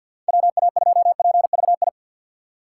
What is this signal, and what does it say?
Morse code, heard as a keyed tone, about 52 wpm, 700 Hz. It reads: WA2PVI
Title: USA Call Signs - 6 Characters - 6x Character Spacing - 50wpm